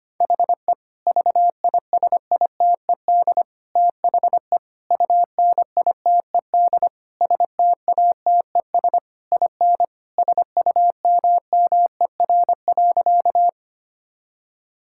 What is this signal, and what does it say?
Morse code, heard as a keyed tone, about 25 wpm, 700 Hz. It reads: HE VISITED THE UNITED STATES IN SUMMER.
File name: practice / 25wpm / Sentences with 4-Word Phrases - 25wpm